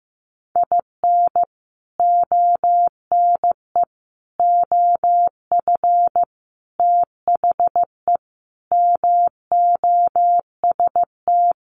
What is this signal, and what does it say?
Morse code, heard as a keyed tone, about 15 wpm, 700 Hz. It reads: IN ONE OF THE MOST
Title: Sets of 5 Words Encore 15wpm